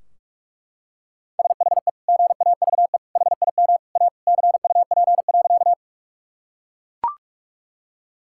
Morse code, heard as a keyed tone, about 41 wpm, 700 Hz. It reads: SHE GAVE HIM A CUP.
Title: Sentences from Top 1600 Words 45wpm